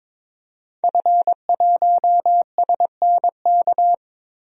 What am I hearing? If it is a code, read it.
Morse code, heard as a keyed tone, about 22 wpm, 700 Hz. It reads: F1SNK
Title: DXCC-100 Call Signs - 5 Characters - 22wpm